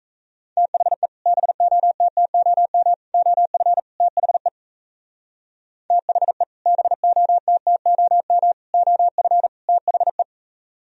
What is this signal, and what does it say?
Morse code, heard as a keyed tone, about 35 wpm, 700 Hz. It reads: THE BOTTOM OF THE THE BOTTOM OF THE